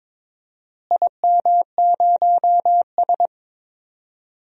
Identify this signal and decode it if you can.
Morse code, heard as a keyed tone, about 22 wpm, 700 Hz. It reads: IM0S
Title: DXCC-100 Call Signs - 4 Characters - Repeated 2x - 22wpm